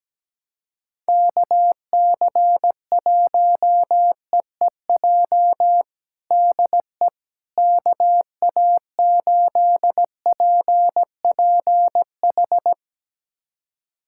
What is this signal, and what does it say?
Morse code, heard as a keyed tone, about 17 wpm, 700 Hz. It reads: KC1EEJ DE KA8PPH
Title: USA Callsign DE Callsign - 6 Characters - 17wpm